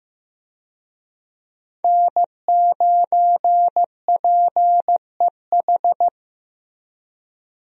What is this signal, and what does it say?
Morse code, heard as a keyed tone, about 15 wpm, 700 Hz. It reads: N9PEH
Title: USA Call Signs - 5 Characters - Repeated 2x - 15wpm